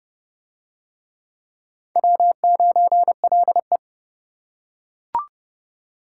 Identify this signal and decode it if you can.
Morse code, heard as a keyed tone, about 28 wpm, 700 Hz. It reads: W9LE